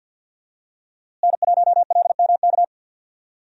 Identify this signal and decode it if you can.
Morse code, heard as a keyed tone, about 46 wpm, 700 Hz. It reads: N1LMK
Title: USA Call Signs - 5 Characters - Repeated 2x - 50wpm